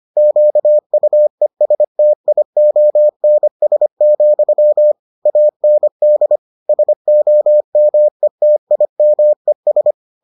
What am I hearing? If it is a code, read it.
Morse code, heard as a keyed tone, about 25 wpm, 600 Hz. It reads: QUESTIONS, AND SOMETIMES